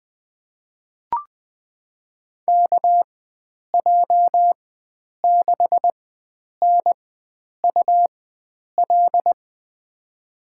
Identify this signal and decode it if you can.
Morse code, heard as a keyed tone, about 20 wpm, spaced as for 9 wpm, 700 Hz. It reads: KJ6NUL
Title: USA Call Signs - 6 Characters - 4x Character Spacing - 20wpm